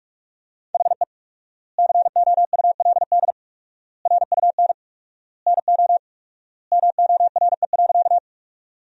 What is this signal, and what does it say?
Morse code, heard as a keyed tone, about 42 wpm, 700 Hz. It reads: HE COULD RUN NO MORE.
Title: 4x Word-Spacing - Sentences from Top 300 Words - 45wpm